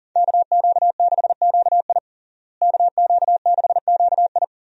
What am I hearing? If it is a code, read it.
Morse code, heard as a keyed tone, about 40 wpm, 700 Hz. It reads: KQ6QI KQ6QI
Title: USA Call Signs - 5 Characters - Repeated 2x - 40wpm